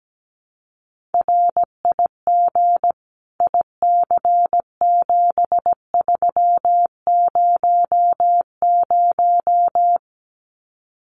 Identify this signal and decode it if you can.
Morse code, heard as a keyed tone, about 17 wpm, 700 Hz. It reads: RIG IC7300
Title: Common QSO Phrases Encore 17wpm